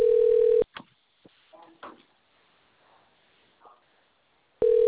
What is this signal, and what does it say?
Ringback tone (North America), 440 + 480 Hz.